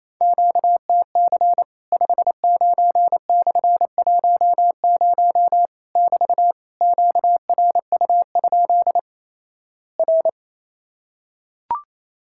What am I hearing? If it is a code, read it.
Morse code, heard as a keyed tone, about 28 wpm, 700 Hz. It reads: QTC 59/10 <BT> QRU? R
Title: Worked All Europe DX Contest - QTC Exchange - 28wpm